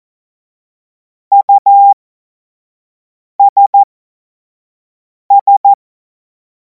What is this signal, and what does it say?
Morse code, heard as a keyed tone, about 14 wpm, spaced as for 5 wpm, 800 Hz. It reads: USS
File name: Morsekurs / Witze / Witz 17. Februar 2026 25 F70